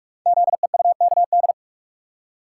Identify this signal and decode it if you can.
Morse code, heard as a keyed tone, about 44 wpm, 690 Hz. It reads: ZEUKD
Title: Code Groups - 5 Letters - 45wpm